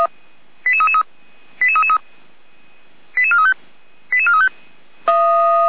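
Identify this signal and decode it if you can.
Five-tone selective call (ZVEI), heard as quick runs of five tones, tones of about 70 ms. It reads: 88333 88333 88435 88435
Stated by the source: ZVEI five-tone call received by SDR, on the audio server of Feuerwehr Orlamünde (fire brigade).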